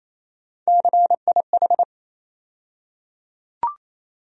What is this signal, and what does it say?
Morse code, heard as a keyed tone, about 28 wpm, 700 Hz. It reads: CIH
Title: USA Call Sign Postfixes - 3 Letters 28wpm